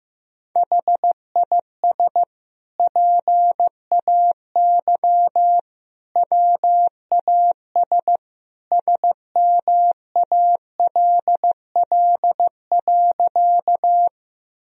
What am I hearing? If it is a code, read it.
Morse code, heard as a keyed tone, about 15 wpm, 700 Hz. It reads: HIS PAY WAS SMALL.
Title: Sentences from Top 1000 Words - Review Part 1 - 15wpm